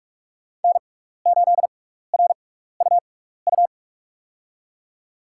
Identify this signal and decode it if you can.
Morse code, heard as a keyed tone, about 41 wpm, spaced as for 15 wpm, 700 Hz. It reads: N8RUU
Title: USA Call Signs - 6x Character Spacing - 45wpm